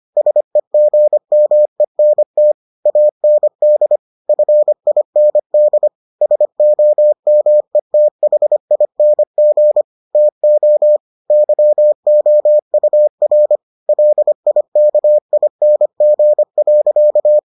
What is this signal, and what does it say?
Morse code, heard as a keyed tone, about 25 wpm, 600 Hz. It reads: SEGMENT AND FIND SOMETHING TO YOUR LIKING.